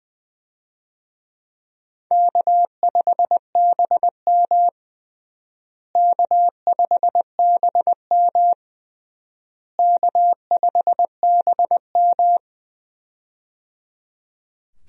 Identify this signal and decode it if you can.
Morse code, heard as a keyed tone, about 20 wpm, 700 Hz. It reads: K5BM K5BM K5BM
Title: USA Call Signs - 4 Characters - Repeated 3x - 20wpm